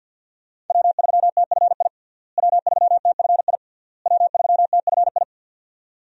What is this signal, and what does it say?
Morse code, heard as a keyed tone, about 51 wpm, 700 Hz. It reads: W3TFI W3TFI W3TFI
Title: USA Call Signs - 5 Characters - Repeated 3x - 50wpm